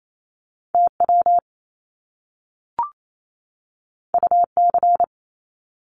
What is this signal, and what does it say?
Morse code, heard as a keyed tone, about 28 wpm, 700 Hz. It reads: TW UC
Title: Top Two Letter Combinations in Words 28wpm